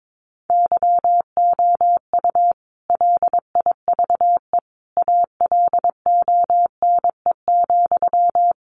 Morse code, heard as a keyed tone, about 22 wpm, 700 Hz. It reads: YOU LIVE ALONE,
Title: Sentences from Top 1000 Words 22wpm